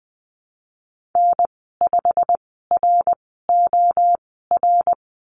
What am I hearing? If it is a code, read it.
Morse code, heard as a keyed tone, about 20 wpm, spaced as for 15 wpm, 700 Hz. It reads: N5ROR
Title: 167 Lesson 27g - Number 7 - Call Signs 20wpm